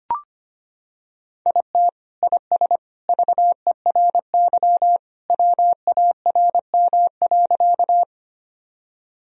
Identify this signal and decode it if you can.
Morse code, heard as a keyed tone, about 25 wpm, 700 Hz. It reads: IT IS VERY WARM.
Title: Sentences from Top 1000 Words - Review Part 2 - 25wpm